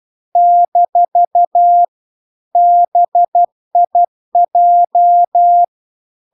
Morse code, heard as a keyed tone, about 12 wpm, 700 Hz. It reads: - BIJ
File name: cw-cursus-VERON-a03